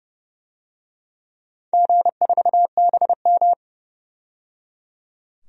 Morse code, heard as a keyed tone, about 30 wpm, 700 Hz. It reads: G4BM
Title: DXCC-100 Call Signs - 4 Characters - 30wpm